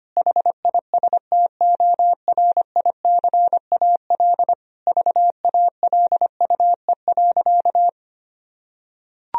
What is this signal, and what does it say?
Morse code, heard as a keyed tone, about 25 wpm, 700 Hz. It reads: HISTORICAL VALUE.
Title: Sentences from Top 1000 Words - Review Part 4 - 25wpm